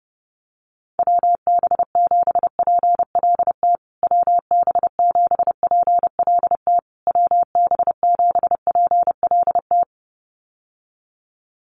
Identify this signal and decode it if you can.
Morse code, heard as a keyed tone, about 30 wpm, 700 Hz. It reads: WB7PLT WB7PLT WB7PLT
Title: USA Call Signs Repeated 3x 30wpm